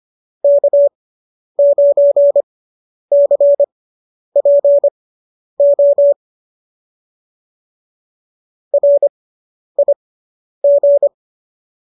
Morse code, heard as a keyed tone, about 25 wpm, spaced as for 10 wpm, 570 Hz. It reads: K9CPO RIG